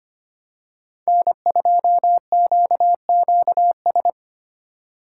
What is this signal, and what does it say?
Morse code, heard as a keyed tone, about 25 wpm, 700 Hz. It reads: N2QQS
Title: USA Call Signs - 5 Characters - 3x Character Spacing - 25wpm